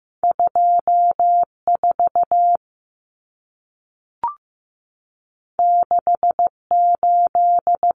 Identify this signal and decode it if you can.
Morse code, heard as a keyed tone, about 15 wpm, 700 Hz. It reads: 24 68
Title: Numbers - 2-Digit 15wpm